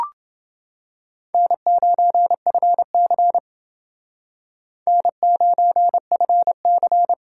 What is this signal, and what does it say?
Morse code, heard as a keyed tone, about 28 wpm, 700 Hz. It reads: N9FC N9FC